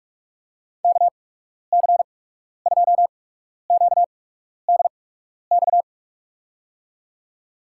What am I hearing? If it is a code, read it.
Morse code, heard as a keyed tone, about 47 wpm, spaced as for 12 wpm, 690 Hz. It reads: KC2QDX
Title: USA Call Signs - 8x Character Spacing - 45wpm